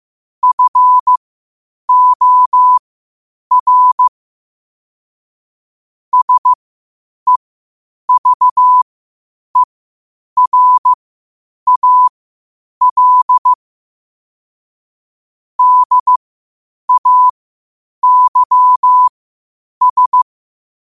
Morse code, heard as a keyed tone, about 15 wpm, spaced as for 8 wpm, 1000 Hz. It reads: FOR SEVERAL DAYS
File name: qotdfiles / 8wpm